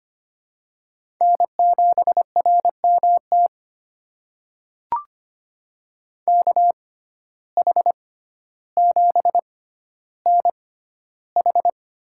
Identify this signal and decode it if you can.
Morse code, heard as a keyed tone, about 25 wpm, spaced as for 13 wpm, 700 Hz. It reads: N7RMT KH7NH
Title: USA Call Signs - 6x Character Spacing - 25wpm